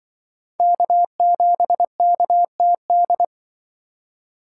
Morse code, heard as a keyed tone, about 23 wpm, 700 Hz. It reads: K7KTD
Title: Speed Racing - USA 1x3 Call Signs 20wpm